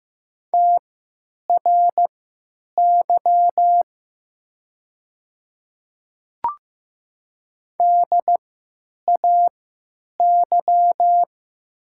Morse code, heard as a keyed tone, about 15 wpm, spaced as for 9 wpm, 700 Hz. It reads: TRY DAY
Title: 3x Character Spacing - Top 200 Words - 15wpm